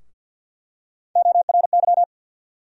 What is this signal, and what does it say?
Morse code, heard as a keyed tone, about 49 wpm, 700 Hz. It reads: ORY
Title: USA Call Sign Postfixes - 3 Letters 50wpm